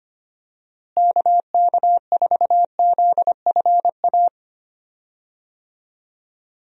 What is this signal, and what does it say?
Morse code, heard as a keyed tone, about 25 wpm, 700 Hz. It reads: KK4ZFA
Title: USA Call Signs - 6 Characters - Repeated 3x - 25wpm